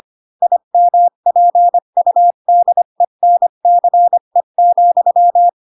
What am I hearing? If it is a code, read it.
Morse code, heard as a keyed tone, about 25 wpm, 700 Hz. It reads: IMPUDENCE,
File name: Pinocchio / Pinocchio ch4 20 25 ik1hge